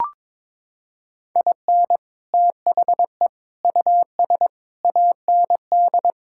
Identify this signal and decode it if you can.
Morse code, heard as a keyed tone, about 22 wpm, 700 Hz. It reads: IN THE US AND
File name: practice / 22wpm / Sets of 7 Words - 22wpm